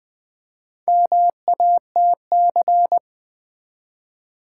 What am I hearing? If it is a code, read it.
Morse code, heard as a keyed tone, about 20 wpm, 700 Hz. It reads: MATC
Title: Build A Word - 1 Letter At A Time - 5-Letter Words - 20wpm